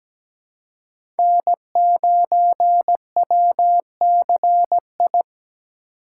Morse code, heard as a keyed tone, about 17 wpm, 700 Hz. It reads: N9WCI
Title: USA Call Signs - 3x Character Spacing - 17wpm